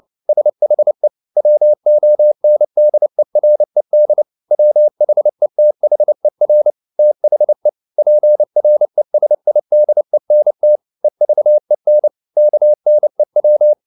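Morse code, heard as a keyed tone, about 29 wpm, 600 Hz. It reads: SHE WONDERED WHETHER THE PRESIDENT EVEN KNEW
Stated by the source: Morse code at 29 wpm, sending Ken Follett's novel Never in English, from CW med Gnister.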